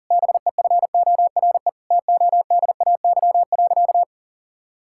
Morse code, heard as a keyed tone, about 40 wpm, 700 Hz. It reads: BEFORE TODAY.
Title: Sentences from Top 1000 Words - Review Part 6 - 40wpm